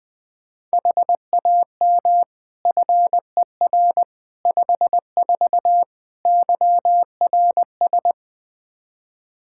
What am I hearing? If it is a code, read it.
Morse code, heard as a keyed tone, about 20 wpm, 700 Hz. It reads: HAM FER 54 YRS
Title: K1USN SST QSO Practice 20wpm